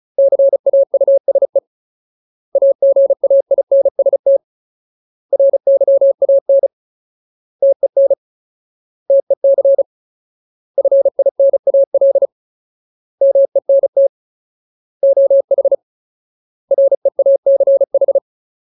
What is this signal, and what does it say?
Morse code, heard as a keyed tone, about 34 wpm, 550 Hz. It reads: CAUSE AGAINST RYAN TEN TEC FINAL MENT OH REACH
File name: Ultimate Word List / 35WPM Ultimate